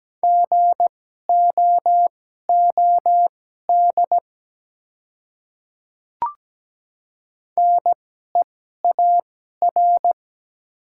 Morse code, heard as a keyed tone, about 17 wpm, spaced as for 12 wpm, 700 Hz. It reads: GOOD NEAR
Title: 2x Character Spacing - Top 200 Words - 17wpm